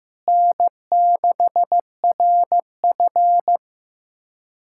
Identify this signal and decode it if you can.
Morse code, heard as a keyed tone, about 15 wpm, 700 Hz. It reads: N6RF
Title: USA Callsign DE Callsign - 4 Characters - 15wpm